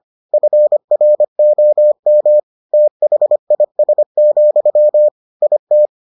Morse code, heard as a keyed tone, about 25 wpm, 600 Hz. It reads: FROM THIS, IT